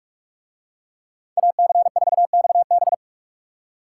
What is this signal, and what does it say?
Morse code, heard as a keyed tone, about 44 wpm, 700 Hz. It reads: AC4XB